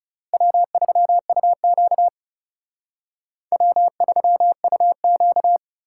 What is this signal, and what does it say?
Morse code, heard as a keyed tone, about 37 wpm, 700 Hz. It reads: W3UQ W3UQ
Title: Speed Racing - USA 1x2 Call Signs 30wpm